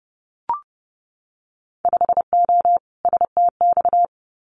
Morse code, heard as a keyed tone, about 30 wpm, 700 Hz. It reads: 5O STX
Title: Winter Field Day Exchanges 30wpm